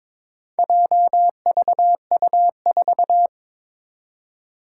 Morse code, heard as a keyed tone, about 22 wpm, 700 Hz. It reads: JVU4
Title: Four Challenging Letter-Number Combinations 22wpm